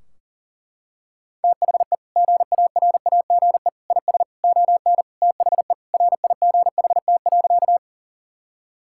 Morse code, heard as a keyed tone, about 40 wpm, 700 Hz. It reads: THE GARAGE IS ON THE RIGHT.
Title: Sentences from Top 1600 Words 40wpm